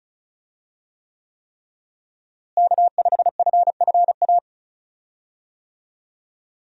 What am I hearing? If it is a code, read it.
Morse code, heard as a keyed tone, about 36 wpm, 700 Hz. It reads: K5FFA
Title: USA Call Signs - 5 Characters - Repeated 3x - 35wpm